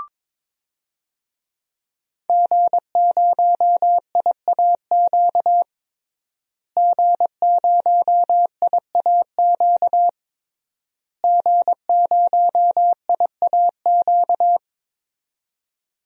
Morse code, heard as a keyed tone, about 21 wpm, 700 Hz. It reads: G0IAQ G0IAQ G0IAQ
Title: DXCC-100 Call Signs - 5 Characters - Repeated 3x - 22wpm